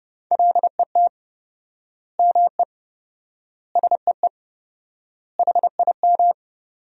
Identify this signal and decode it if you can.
Morse code, heard as a keyed tone, about 30 wpm, 700 Hz. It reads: LET ME SEE HIM
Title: 4x Word-Spacing - Sentences from Top 300 Words - 30wpm